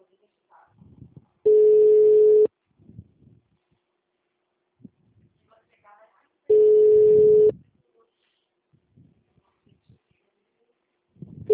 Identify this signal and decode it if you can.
Ringback tone (Europe, 425 Hz), with speech.
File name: out- -2008- - - .13563